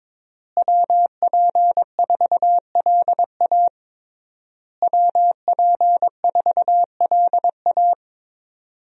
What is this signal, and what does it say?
Morse code, heard as a keyed tone, about 22 wpm, 700 Hz. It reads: WP4LA WP4LA
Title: USA Call Signs - 5 Characters - Repeated 3x - 22wpm